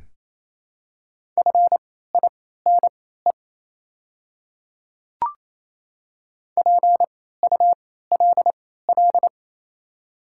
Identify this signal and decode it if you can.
Morse code, heard as a keyed tone, about 28 wpm, spaced as for 16 wpm, 700 Hz. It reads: FINE PULL
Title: 3x Character Spacing - Top 500 Words - 28wpm